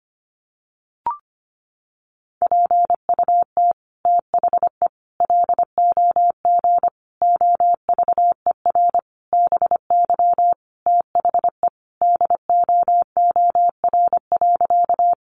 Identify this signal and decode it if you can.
Morse code, heard as a keyed tone, about 25 wpm, 700 Hz. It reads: PUT THE LOG OVER BY THE DOOR.
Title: Sentences from Top 1400 Words 25wpm